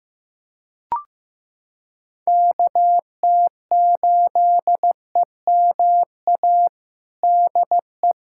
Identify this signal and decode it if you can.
Morse code, heard as a keyed tone, about 15 wpm, 700 Hz. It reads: KT8EMA DE
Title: USA Callsign DE Callsign 15wpm